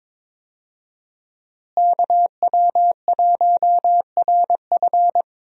Morse code, heard as a keyed tone, about 22 wpm, 700 Hz. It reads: KW1RF
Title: USA Call Signs - 5 Characters - Repeated 2x - 22wpm